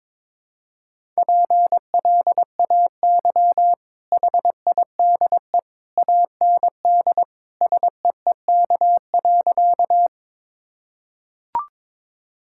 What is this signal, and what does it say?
Morse code, heard as a keyed tone, about 21 wpm, 700 Hz. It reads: PLAY HIDE AND SEEK.
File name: practice / 22wpm / Easy Sentences with Hints - 22wpm